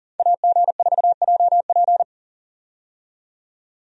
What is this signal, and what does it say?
Morse code, heard as a keyed tone, about 40 wpm, 700 Hz. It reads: AG4JP
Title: USA Call Signs - 5 Characters - 40wpm